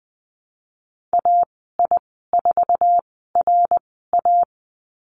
Morse code, heard as a keyed tone, about 20 wpm, spaced as for 15 wpm, 700 Hz. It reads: AI4RA
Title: 055 Lesson 10g - Letter r - Call Signs 20wpm